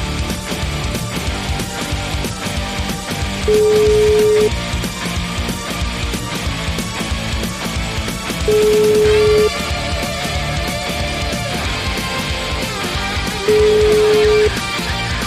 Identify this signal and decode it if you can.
Ringback tone (Europe, 425 Hz), with noise.